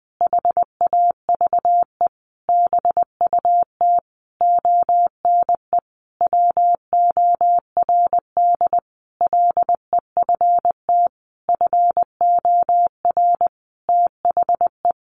Morse code, heard as a keyed tone, about 19 wpm, 700 Hz. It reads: HAVE BUT ONE WORD LEFT FOR THE
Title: Sentences from Top 1500 Words 20wpm